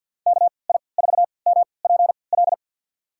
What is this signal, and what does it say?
Morse code, heard as a keyed tone, about 52 wpm, spaced as for 29 wpm, 690 Hz. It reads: KI4MPL
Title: USA Call Signs - 6 Characters - 3x Character Spacing - 50wpm